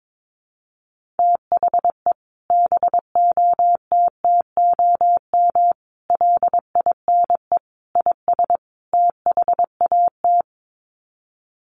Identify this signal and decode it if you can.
Morse code, heard as a keyed tone, about 22 wpm, 700 Hz. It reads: THE BOTTOM LINE IS THAT
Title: Sets of 5 Words Encore 22wpm